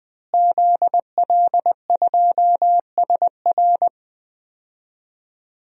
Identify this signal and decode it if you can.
Morse code, heard as a keyed tone, about 20 wpm, 700 Hz. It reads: ZL2SR
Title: DXCC-100 Call Signs - 5 Characters - Repeated 3x - 20wpm